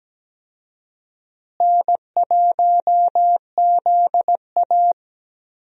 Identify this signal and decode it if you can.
Morse code, heard as a keyed tone, about 17 wpm, 700 Hz. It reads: N1ZA
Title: USA Call Signs - 4 Characters - Repeated 2x - 17wpm